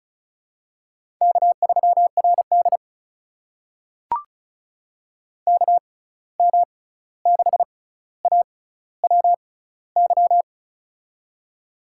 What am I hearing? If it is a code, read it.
Morse code, heard as a keyed tone, about 36 wpm, spaced as for 16 wpm, 700 Hz. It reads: K3RD KM6AWY
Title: USA Call Signs - 6x Character Spacing - 35wpm